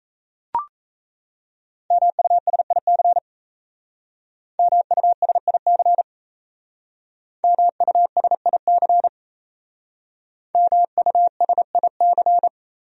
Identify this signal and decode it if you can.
Morse code, heard as a keyed tone, about 35 wpm, 700 Hz. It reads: MUSIC MUSIC MUSIC MUSIC
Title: Speed-Racing Top 500 Words 28wpm